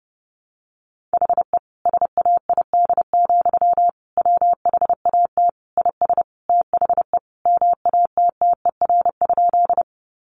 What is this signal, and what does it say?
Morse code, heard as a keyed tone, about 30 wpm, 700 Hz. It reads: HE SAID, WHAT IS THE MATTER?
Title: Sentences from Top 700 Words 30wpm